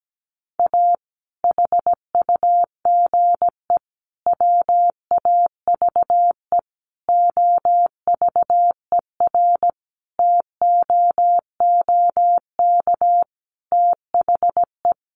Morse code, heard as a keyed tone, about 17 wpm, 700 Hz. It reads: A HUGE WAVE OVERTOOK THE
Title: Sentences from Top 1400 Words 17wpm